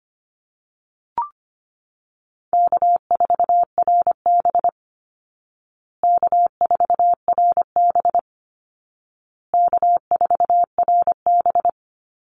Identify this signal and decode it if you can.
Morse code, heard as a keyed tone, about 25 wpm, 700 Hz. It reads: K4RB K4RB K4RB
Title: Easy Call Signs Repeated 3x - 25wpm